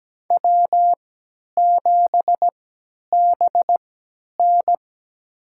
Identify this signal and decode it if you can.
Morse code, heard as a keyed tone, about 17 wpm, spaced as for 10 wpm, 700 Hz. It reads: W7BN
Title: USA Call Signs - 4 Characters - 3x Character Spacing - 17wpm